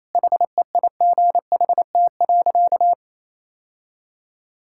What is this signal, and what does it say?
Morse code, heard as a keyed tone, about 28 wpm, 700 Hz. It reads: HEIGHT.
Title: Sentences from Top 1600 Words 28wpm